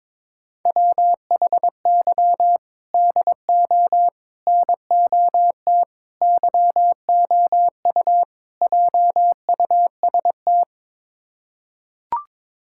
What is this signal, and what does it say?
Morse code, heard as a keyed tone, about 22 wpm, 700 Hz. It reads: WHY DO NOT YOU JUST
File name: practice / 22wpm / Build A Phrase - 5-Word Phrases - 22wpm